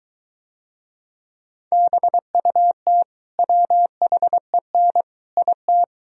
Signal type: Morse code

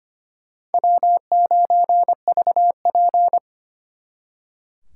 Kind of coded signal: Morse code

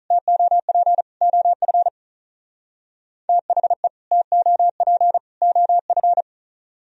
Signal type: Morse code

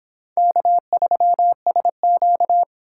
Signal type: Morse code

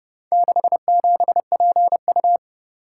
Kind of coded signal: Morse code

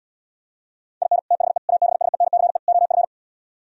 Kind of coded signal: Morse code

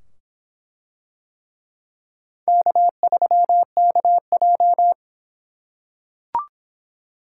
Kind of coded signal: Morse code

